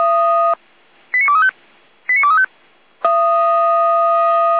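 Five-tone selective call (ZVEI), heard as quick runs of five tones, tones of about 70 ms. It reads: 89235 89235
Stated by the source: ZVEI five-tone selective call received on SDR.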